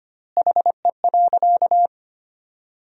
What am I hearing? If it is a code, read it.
Morse code, heard as a keyed tone, about 25 wpm, 700 Hz. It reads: HE.
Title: 4x Word-Spacing - Sentences from Top 100 Words - 25wpm